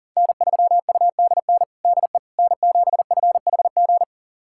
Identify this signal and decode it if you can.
Morse code, heard as a keyed tone, about 37 wpm, 700 Hz. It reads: N3UDN DE N7FHG
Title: USA Callsign DE Callsign - 5 Characters - 40wpm